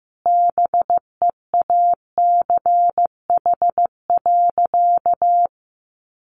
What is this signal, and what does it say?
Morse code, heard as a keyed tone, about 14 wpm, 700 Hz. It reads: BEACH.
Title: Sentences from Top 1500 Words 15wpm